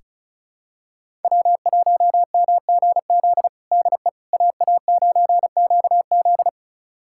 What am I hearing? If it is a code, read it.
Morse code, heard as a keyed tone, about 36 wpm, 700 Hz. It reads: W1MGZ DE AA9QZ
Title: USA Callsign DE Callsign - 5 Characters - 35wpm